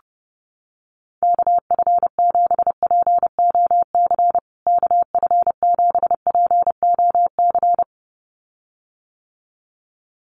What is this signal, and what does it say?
Morse code, heard as a keyed tone, about 30 wpm, 700 Hz. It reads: KF7POC KF7POC
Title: USA Call Signs Repeated 2x 30wpm